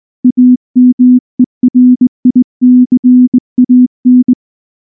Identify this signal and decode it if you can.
Morse code, heard as a keyed tone, about 20 wpm, 250 Hz. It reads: AMERICAN